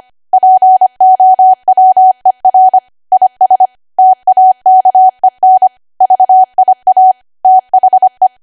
Morse code, heard as a keyed tone, about 25 wpm, 750 Hz. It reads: POWER IS TAKEN VIA THE
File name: cwa / W1AW-Level3 / 25WPM